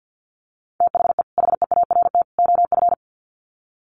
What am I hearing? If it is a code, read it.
Morse code, heard as a keyed tone, about 50 wpm, 700 Hz. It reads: THE HEART OF
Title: Sets of 3 Words 50wpm